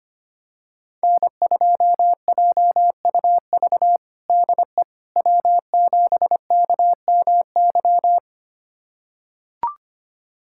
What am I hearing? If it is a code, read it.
Morse code, heard as a keyed tone, about 25 wpm, 700 Hz. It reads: N2JUV DE W7KMY
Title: USA Callsign DE Callsign - 5 Characters - 25wpm